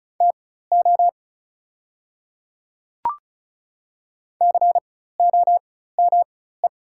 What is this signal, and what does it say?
Morse code, heard as a keyed tone, about 36 wpm, spaced as for 16 wpm, 700 Hz. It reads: TO COME
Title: 4x Character Spacing - Top 100 Words - 35wpm